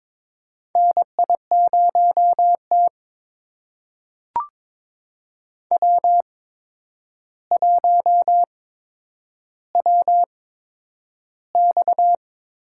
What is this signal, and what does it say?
Morse code, heard as a keyed tone, about 22 wpm, spaced as for 9 wpm, 700 Hz. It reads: NI0T W1WX
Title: USA Call Signs - 4 Characters - 8x Character Spacing - 22wpm